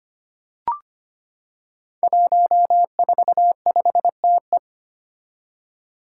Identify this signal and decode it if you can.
Morse code, heard as a keyed tone, about 25 wpm, 700 Hz. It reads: 145TE
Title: Five Challenging Letter-Number Combinations 25wpm